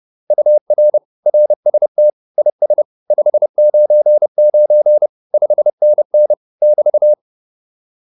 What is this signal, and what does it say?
Morse code, heard as a keyed tone, about 30 wpm, 600 Hz. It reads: UR RST IS 599 5NN <BT>
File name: QSO206 30